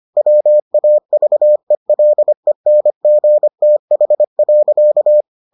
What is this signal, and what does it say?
Morse code, heard as a keyed tone, about 25 wpm, 600 Hz. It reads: WAVELENGTH.